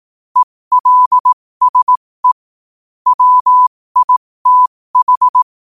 Morse code, heard as a keyed tone, about 18 wpm, spaced as for 15 wpm, 1000 Hz. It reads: ELSE WITH
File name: qotdfiles / 15wpm